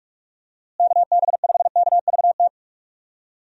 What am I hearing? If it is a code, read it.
Morse code, heard as a keyed tone, about 46 wpm, 700 Hz. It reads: KB5KVT
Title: USA Call Signs - 6 Characters - Repeated 3x - 45wpm